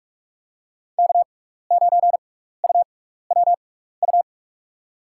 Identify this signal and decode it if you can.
Morse code, heard as a keyed tone, about 44 wpm, spaced as for 15 wpm, 700 Hz. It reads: K9UWU
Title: USA Call Signs - 5 Characters - 6x Character Spacing - 45wpm